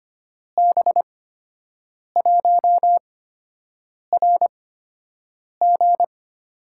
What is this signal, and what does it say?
Morse code, heard as a keyed tone, about 25 wpm, spaced as for 7 wpm, 700 Hz. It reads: B1RG
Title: USA Call Signs - 6 Characters - 8x Character Spacing - 25wpm